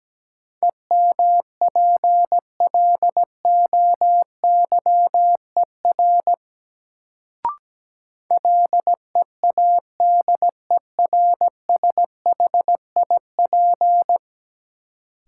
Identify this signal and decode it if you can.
Morse code, heard as a keyed tone, about 17 wpm, 700 Hz. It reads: EMPLOYER LEADERSHIP
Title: Sentences from Top 1600 Words - New Word Drill Alt - 17wpm